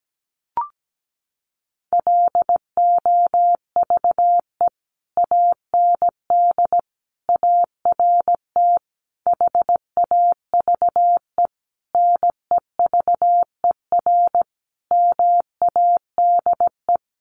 Morse code, heard as a keyed tone, about 17 wpm, 700 Hz. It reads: LOVE AND ART HAVE NEVER MADE
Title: Sentences from Top 1100 Words 17wpm